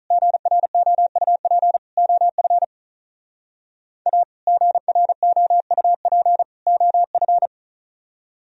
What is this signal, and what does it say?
Morse code, heard as a keyed tone, about 45 wpm, 700 Hz. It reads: GROUP OF A GROUP OF